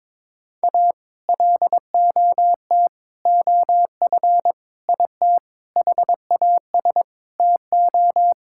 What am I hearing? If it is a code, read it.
Morse code, heard as a keyed tone, about 22 wpm, 700 Hz. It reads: A LOT OF IT HAS TO
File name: practice / 22wpm / Sets of 8 Words - 22wpm